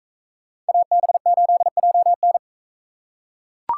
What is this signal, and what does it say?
Morse code, heard as a keyed tone, about 40 wpm, 700 Hz. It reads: AB8JN